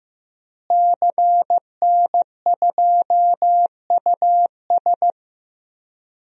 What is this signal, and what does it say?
Morse code, heard as a keyed tone, about 15 wpm, 700 Hz. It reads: CN2US
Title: DXCC-100 Call Signs - 5 Characters - Repeated 2x - 15wpm